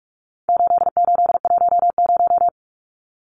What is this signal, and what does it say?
Morse code, heard as a keyed tone, about 44 wpm, 700 Hz. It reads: 8810
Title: Numbers - 4-Digit 45wpm